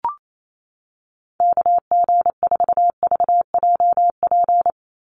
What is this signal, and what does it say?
Morse code, heard as a keyed tone, about 28 wpm, 700 Hz. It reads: KG4VJP
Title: USA Call Signs 28wpm